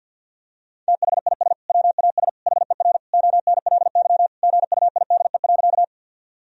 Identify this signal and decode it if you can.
Morse code, heard as a keyed tone, about 51 wpm, 700 Hz. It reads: THIS WAS HER ONLY GUIDE.
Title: Sentences from Top 1000 Words - Review Part 3 - 50wpm